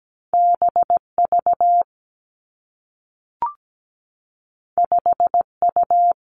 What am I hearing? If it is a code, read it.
Morse code, heard as a keyed tone, about 17 wpm, 700 Hz. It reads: BV 5U
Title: Two Challenging Letter-Number Combinations 17wpm